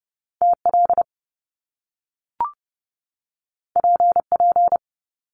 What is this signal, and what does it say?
Morse code, heard as a keyed tone, about 30 wpm, 700 Hz. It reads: TL PP
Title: Top Two Letter Combinations in Words 30wpm